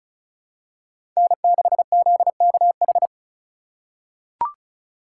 Morse code, heard as a keyed tone, about 36 wpm, 700 Hz. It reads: N6ZKH
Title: USA Call Signs - 5 Characters - 2x Character Spacing - 35wpm